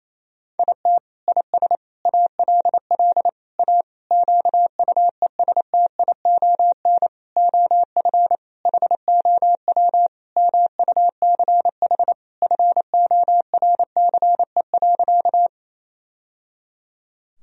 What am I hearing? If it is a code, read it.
Morse code, heard as a keyed tone, about 28 wpm, 700 Hz. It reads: IT IS ALL A QUESTION OF HOW MUCH FORCE.
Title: Sentences from Top 1000 Words - Review Part 3 - 28wpm